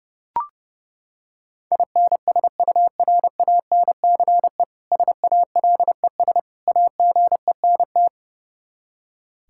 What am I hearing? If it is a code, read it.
Morse code, heard as a keyed tone, about 31 wpm, 700 Hz. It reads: INSURANCE SALES AGENT
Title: Practice QSO - Exchange 3 - More Info - 30wpm